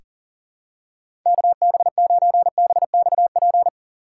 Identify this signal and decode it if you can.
Morse code, heard as a keyed tone, about 40 wpm, 700 Hz. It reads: KB9BXP
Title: USA Call Signs - 6 Characters - 6x Character Spacing - 40wpm